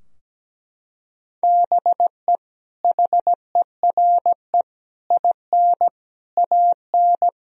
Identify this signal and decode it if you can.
Morse code, heard as a keyed tone, about 17 wpm, 700 Hz. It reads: BE HERE IN AN
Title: Sentences from Top 1000 Words - Review Part 4 - 17wpm